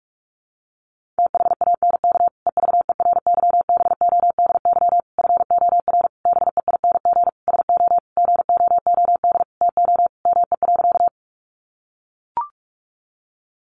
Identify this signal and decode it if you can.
Morse code, heard as a keyed tone, about 44 wpm, 700 Hz. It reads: THANK EVERYBODY FOR BEING SO GOOD TO ME.
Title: Sentences from Top 1400 Words 45wpm